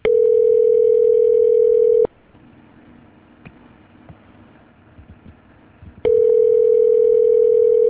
Ringback tone (North America), 440 + 480 Hz.